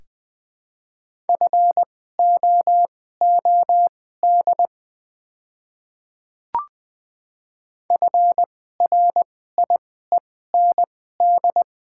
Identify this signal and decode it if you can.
Morse code, heard as a keyed tone, about 20 wpm, spaced as for 14 wpm, 700 Hz. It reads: FOOD FRIEND
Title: 2x Character Spacing - Top 300 Words - 20wpm